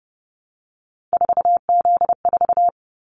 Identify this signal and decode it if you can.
Morse code, heard as a keyed tone, about 30 wpm, 700 Hz. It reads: 4Z4
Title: Callsign Prefixes for Top 50 DXCC - 30wpm